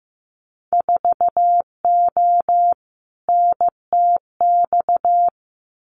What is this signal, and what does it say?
Morse code, heard as a keyed tone, about 15 wpm, 700 Hz. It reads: 4O NTX
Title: Winter Field Day Exchanges 15wpm